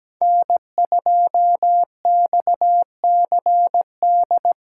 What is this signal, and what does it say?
Morse code, heard as a keyed tone, about 17 wpm, 700 Hz. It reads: N2XCD
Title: USA Call Signs - 5 Characters - 2x Character Spacing - 17wpm